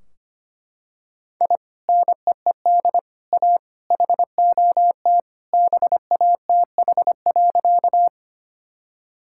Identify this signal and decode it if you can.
Morse code, heard as a keyed tone, about 25 wpm, 700 Hz. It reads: I NEED A HOT BATH.
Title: Sentences from Top 1600 Words 25wpm